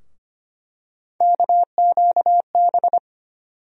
Morse code, heard as a keyed tone, about 25 wpm, 700 Hz. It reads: KQB
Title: USA Call Sign Postfixes - 3 Letters 25wpm